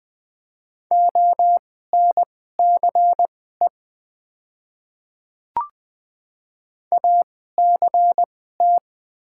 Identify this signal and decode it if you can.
Morse code, heard as a keyed tone, about 20 wpm, spaced as for 14 wpm, 700 Hz. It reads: ONCE ACT
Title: 2x Character Spacing - Top 500 Words - 20wpm